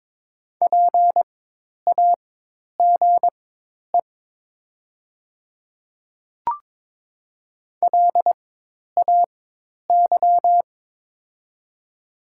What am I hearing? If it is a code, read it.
Morse code, heard as a keyed tone, about 22 wpm, spaced as for 10 wpm, 700 Hz. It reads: PAGE LAY
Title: 4x Character Spacing - Top 500 Words - 22wpm